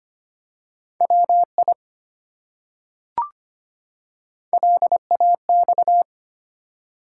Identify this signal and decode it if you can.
Morse code, heard as a keyed tone, about 25 wpm, 700 Hz. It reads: WI LAX
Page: morse-code-ninja-practice-25wpm